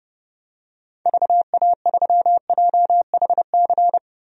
Morse code, heard as a keyed tone, about 32 wpm, 700 Hz. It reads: VA3JHC
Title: DXCC-100 Call Signs - 6 Characters - 30wpm